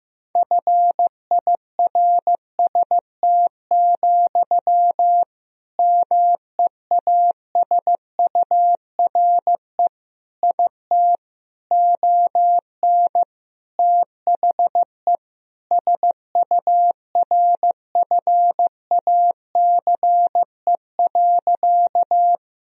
Morse code, heard as a keyed tone, about 15 wpm, 700 Hz. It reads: FIRST, MEASURE IT ON THE SURFACE.
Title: Sentences from Top 1000 Words - Review Part 3 - 15wpm